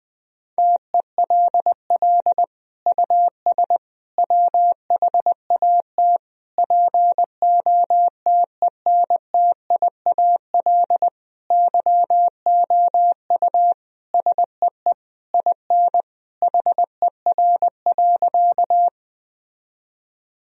Morse code, heard as a keyed tone, about 21 wpm, 700 Hz. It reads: TELL US WHAT POTENTIAL YOU SEE IN HER.
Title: Sentences from Top 1000 Words - Review Part 4 - 20wpm